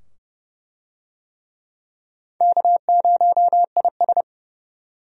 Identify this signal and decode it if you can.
Morse code, heard as a keyed tone, about 30 wpm, 700 Hz. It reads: K0IS
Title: USA Call Signs - 4 Characters - 30wpm